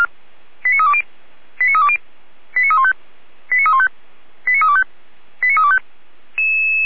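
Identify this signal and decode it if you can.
Five-tone selective call (ZVEI), heard as quick runs of five tones, tones of about 70 ms. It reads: 89320 89320 89315 89315 89325 89325
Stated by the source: Five-tone ZVEI selective call received by SDR, from the feuerwehr-orlamuende.org fire brigade alerting archive.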